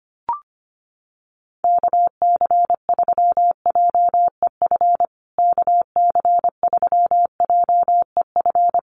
Morse code, heard as a keyed tone, about 25 wpm, 700 Hz. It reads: KC3JEF KC3JEF
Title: USA Call Signs Repeated 3x 25wpm